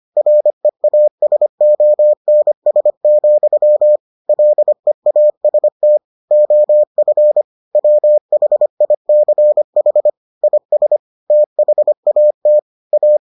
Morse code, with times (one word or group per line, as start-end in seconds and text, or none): 0.2-4.0 REASONS,
4.3-6.0 LEAST
6.3-7.4 OF
7.8-10.1 WHICH
10.4-11.0 IS
11.3-12.6 THAT
12.9-13.2 A